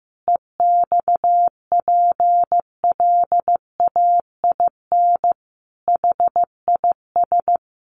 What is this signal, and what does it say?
Morse code, heard as a keyed tone, about 15 wpm, 700 Hz. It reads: EXPLAIN HIS
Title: Sentences from Top 1300 Words 15wpm